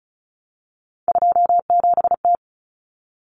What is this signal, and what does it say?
Morse code, heard as a keyed tone, about 34 wpm, 700 Hz. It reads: 27T
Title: Common Cut Numbers - 3-Digit 35wpm